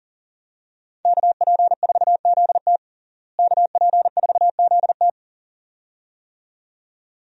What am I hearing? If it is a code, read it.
Morse code, heard as a keyed tone, about 40 wpm, 700 Hz. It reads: KP4ZT KP4ZT
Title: USA Call Signs - 5 Characters - Repeated 2x - 40wpm